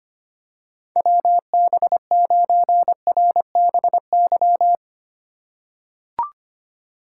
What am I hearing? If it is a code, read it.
Morse code, heard as a keyed tone, about 25 wpm, 700 Hz. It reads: WB9RBY
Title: USA Call Signs - 3x Character Spacing - 25wpm